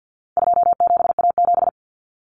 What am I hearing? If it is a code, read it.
Morse code, heard as a keyed tone, about 50 wpm, 700 Hz. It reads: 27A7
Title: Common Cut Numbers - 4-Digit 50wpm